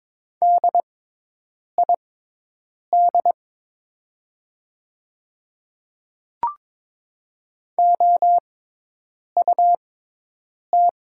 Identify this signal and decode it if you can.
Morse code, heard as a keyed tone, about 22 wpm, spaced as for 8 wpm, 700 Hz. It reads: DID OUT
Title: 6x Character Spacing - Top 200 Words - 22wpm